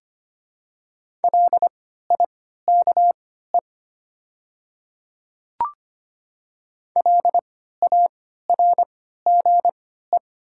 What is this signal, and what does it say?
Morse code, heard as a keyed tone, about 25 wpm, spaced as for 14 wpm, 700 Hz. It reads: LIKE LARGE
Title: 3x Character Spacing - Top 200 Words - 25wpm